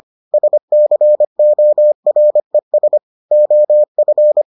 Morse code, heard as a keyed tone, about 25 wpm, 600 Hz. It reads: SCORES OF